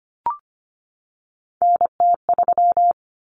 Morse code, heard as a keyed tone, about 25 wpm, 700 Hz. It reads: NT3
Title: Common Cut Numbers - 3-Digit 25wpm